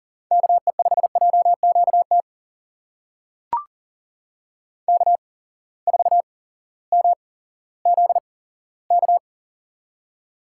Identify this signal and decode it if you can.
Morse code, heard as a keyed tone, about 40 wpm, spaced as for 18 wpm, 700 Hz. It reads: KE5JQT K4MZK
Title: USA Call Signs - 8x Character Spacing - 40wpm